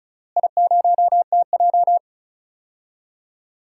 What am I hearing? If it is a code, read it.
Morse code, heard as a keyed tone, about 36 wpm, 700 Hz. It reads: I0TJ
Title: DXCC-100 Call Signs - 4 Characters - Repeated 3x - 35wpm